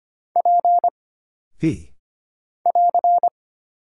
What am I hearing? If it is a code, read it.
Morse code, heard as a keyed tone, about 25 wpm, 700 Hz. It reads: P <AR>
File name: Single Character (Letter:Number:Punctuation:Prosign) - Mind-Melt - 25wpm